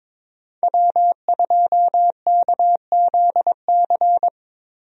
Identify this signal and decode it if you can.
Morse code, heard as a keyed tone, about 22 wpm, 700 Hz. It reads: W2KZC
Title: USA Call Signs - 5 Characters - Repeated 3x - 22wpm